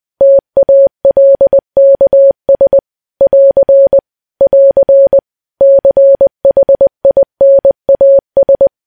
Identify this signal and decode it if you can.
Morse code, heard as a keyed tone, about 20 wpm, 550 Hz. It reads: TALKS <AR> <AR> CHINAS
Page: news-headlines-in-morse-code-at-20-wpm